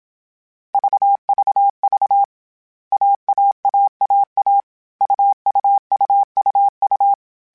Morse code, heard as a keyed tone, about 27 wpm, 800 Hz. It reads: VVV AAAAA UUUUU
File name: Short Runs / Short Runs-mp3s / AUV AUV4 @25